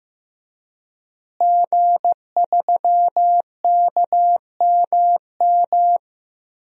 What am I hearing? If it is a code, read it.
Morse code, heard as a keyed tone, about 15 wpm, 700 Hz. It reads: G3KMM
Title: DXCC-100 Call Signs - 5 Characters - Repeated 3x - 15wpm